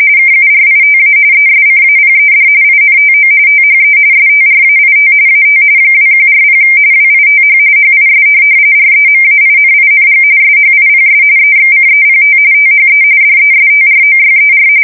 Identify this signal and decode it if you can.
RTTY, about 45.45 baud. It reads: KURITA ESCAPED INTO THE STRAIT AT 2140 WITH HIS REMAINING SHIPS EXCEPT THE DESTROYER
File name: fsk170 / USS-New-Jersey